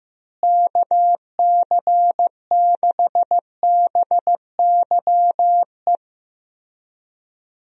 Morse code, heard as a keyed tone, about 16 wpm, 700 Hz. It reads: KC6BYE
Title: USA Call Signs - 6 Characters - Repeated 3x - 15wpm